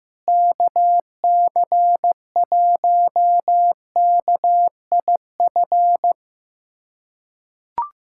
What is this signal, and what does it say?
Morse code, heard as a keyed tone, about 15 wpm, 700 Hz. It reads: KC1KIF
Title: USA Callsign DE Callsign - 6 Characters - 15wpm